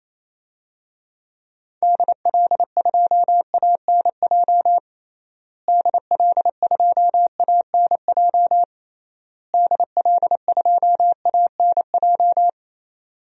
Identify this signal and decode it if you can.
Morse code, heard as a keyed tone, about 28 wpm, 700 Hz. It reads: DL2ANJ DL2ANJ DL2ANJ
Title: DXCC-100 Call Signs - 6 Characters - Repeated 3x - 28wpm